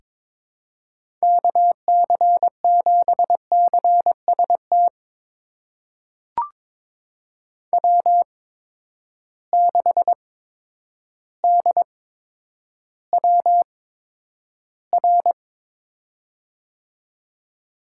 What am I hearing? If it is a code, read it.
Morse code, heard as a keyed tone, about 21 wpm, spaced as for 10 wpm, 700 Hz. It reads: KC7CST W6DWR
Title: USA Call Signs - 8x Character Spacing - 22wpm